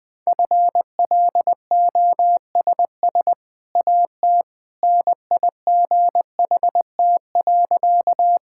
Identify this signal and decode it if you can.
Morse code, heard as a keyed tone, about 20 wpm, 700 Hz. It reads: FLOSS AT NIGHT.
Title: Easy Sentences with Hints - 20wpm